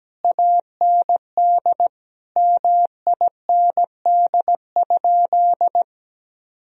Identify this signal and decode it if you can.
Morse code, heard as a keyed tone, about 17 wpm, 700 Hz. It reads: AND MIND?
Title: Sentences from Top 1000 Words - Review Part 2 - 17wpm